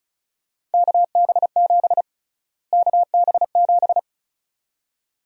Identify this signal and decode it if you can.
Morse code, heard as a keyed tone, about 36 wpm, 700 Hz. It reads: KB7 KB7
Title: USA Call Sign Prefixes - Repeated 2x - 35wpm